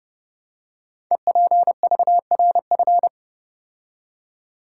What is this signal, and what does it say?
Morse code, heard as a keyed tone, about 29 wpm, 700 Hz. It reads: EPVRF
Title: Code Groups - 5 Letters - 30wpm